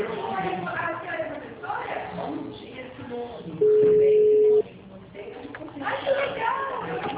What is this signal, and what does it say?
Ringback tone (Europe, 425 Hz), with speech.